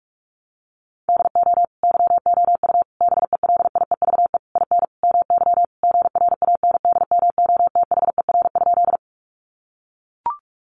Morse code, heard as a keyed tone, about 44 wpm, 700 Hz. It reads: DO YOU BELIEVE IN MY GRANDMOTHER?
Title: Sentences from Top 1200 Words 45wpm